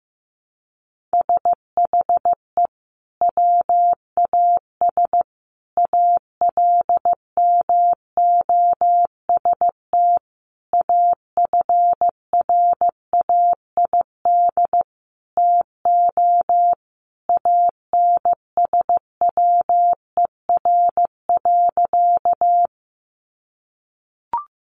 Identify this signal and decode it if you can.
Morse code, heard as a keyed tone, about 15 wpm, 700 Hz. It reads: SHE WAS ALMOST AFRAID TO ANSWER.
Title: Sentences from Top 1500 Words 15wpm